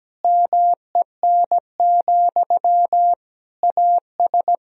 Morse code, heard as a keyed tone, about 17 wpm, 700 Hz. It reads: MEN, AS
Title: Sentences from Top 500 Words and Abbreviations 17wpm